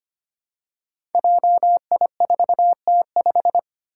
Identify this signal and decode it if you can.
Morse code, heard as a keyed tone, about 25 wpm, 700 Hz. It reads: JI4T5
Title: Five Challenging Letter-Number Combinations 25wpm